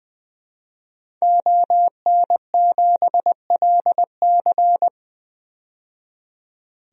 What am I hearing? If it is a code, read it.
Morse code, heard as a keyed tone, about 20 wpm, 700 Hz. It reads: ON7LC
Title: DXCC-100 Call Signs - 5 Characters - 20wpm